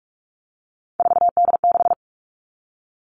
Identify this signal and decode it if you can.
Morse code, heard as a keyed tone, about 44 wpm, 700 Hz. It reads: 4D6
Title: All Cut Numbers - 3-Digit 45wpm